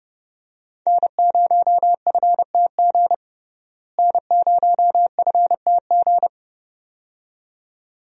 Morse code, heard as a keyed tone, about 30 wpm, 700 Hz. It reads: N0FTG N0FTG
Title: USA Call Signs - 5 Characters - Repeated 2x - 30wpm